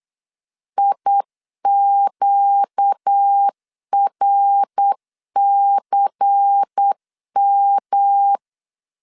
Morse code, heard as a keyed tone, about 8 wpm, 800 Hz. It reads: IQRCM
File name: LEZIONI / 6 PKX0 / 6 PK 8wpm 3003